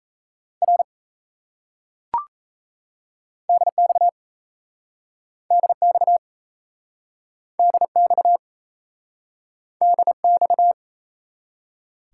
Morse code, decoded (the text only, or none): R DX DX DX DX